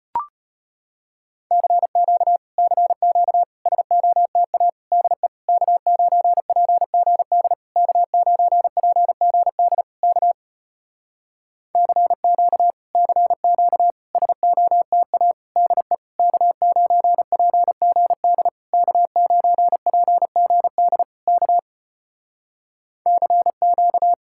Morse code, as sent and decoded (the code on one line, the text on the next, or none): -.-. --.- / -.-. --.- / ... --- - .- / -.. / -.- ----. .--. --. -.. / -.- ----. .--. --. -.. / -.- / -.-. --.- / -.-. --.- / ... --- - .- / -.. / -.- ----. .--. --. -.. / -.- ----. .--. --. -.. / -.- / -.-. --.-
CQ CQ SOTA DE K9PGD K9PGD K CQ CQ SOTA DE K9PGD K9PGD K CQ